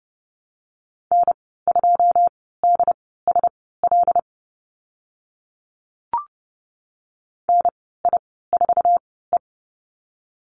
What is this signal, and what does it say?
Morse code, heard as a keyed tone, about 30 wpm, spaced as for 17 wpm, 700 Hz. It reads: N2DSL NI4E
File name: Lesson 14g - Number 2 - Call Signs 30wpm